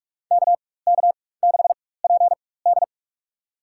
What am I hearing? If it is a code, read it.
Morse code, heard as a keyed tone, about 42 wpm, spaced as for 20 wpm, 700 Hz. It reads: KK6PD
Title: USA Call Signs - 4x Character Spacing - 45wpm